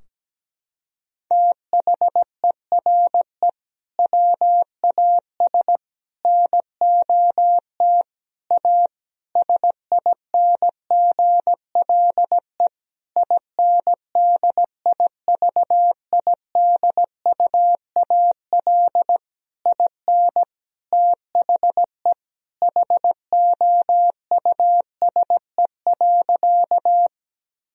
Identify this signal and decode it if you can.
Morse code, heard as a keyed tone, about 17 wpm, 700 Hz. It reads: THERE WAS NOT A SINGLE INDIVIDUAL IN THE HOUSE.
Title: Sentences from Top 1000 Words - Review Part 4 - 17wpm